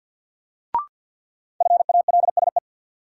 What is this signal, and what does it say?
Morse code, heard as a keyed tone, about 52 wpm, 700 Hz. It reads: FALSE
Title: Sentences from Top 1600 Words - New Word Drill - 50wpm